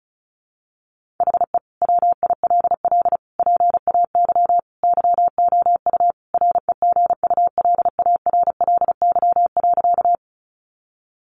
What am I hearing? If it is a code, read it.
Morse code, heard as a keyed tone, about 34 wpm, 700 Hz. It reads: HE WILL PAY YOU REGULARLY.
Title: Sentences from Top 1400 Words 35wpm